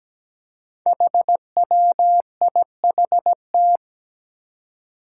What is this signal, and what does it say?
Morse code, heard as a keyed tone, about 17 wpm, 700 Hz. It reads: HWIHT
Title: Five Challenging Letter-Number Combinations 17wpm